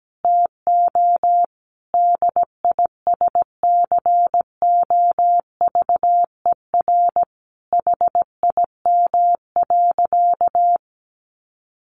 Morse code, heard as a keyed tone, about 17 wpm, 700 Hz. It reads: TO DISCOVER HIM.
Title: Sentences from Top 1100 Words 17wpm